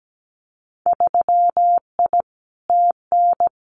Morse code, heard as a keyed tone, about 17 wpm, 690 Hz. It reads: 3I TN
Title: Winter Field Day Exchanges 17wpm